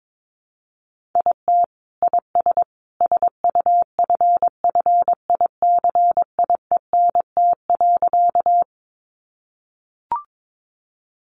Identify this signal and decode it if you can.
Morse code, heard as a keyed tone, about 22 wpm, 700 Hz. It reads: IT IS SUFFICIENT.
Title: Sentences from Top 1400 Words 22wpm